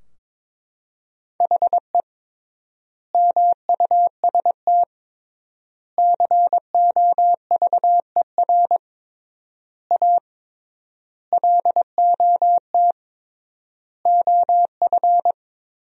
Morse code, heard as a keyed tone, about 22 wpm, 700 Hz. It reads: HE MUST COVER A LOT OF
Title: 3x Word-Spacing - Sentences from Top 500 Words - 22wpm